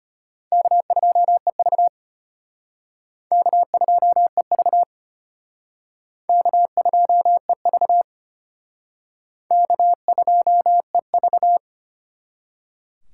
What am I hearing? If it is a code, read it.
Morse code, heard as a keyed tone, about 33 wpm, 700 Hz. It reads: K2EV K2EV K2EV K2EV